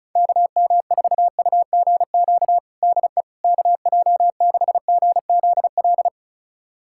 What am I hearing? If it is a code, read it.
Morse code, heard as a keyed tone, about 36 wpm, 700 Hz. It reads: KM4UGQ DE KJ6GZL
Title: USA Callsign DE Callsign 35wpm